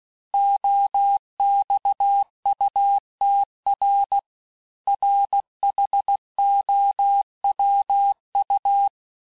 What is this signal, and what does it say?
Morse code, heard as a keyed tone, about 16 wpm, 800 Hz. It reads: OXUTR RHOWU
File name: cours-du-lundi files / cours-1592--teste-chiffre-vitesse-15mpm